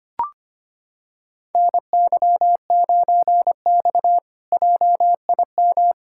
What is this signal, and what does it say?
Morse code, heard as a keyed tone, about 25 wpm, 700 Hz. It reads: NY9X JIM
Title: K1USN SST QSO Practice - 25wpm